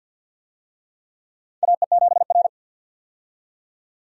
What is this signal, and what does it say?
Morse code, heard as a keyed tone, about 50 wpm, 700 Hz. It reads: AE7R
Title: USA Call Signs - 4 Characters - 50wpm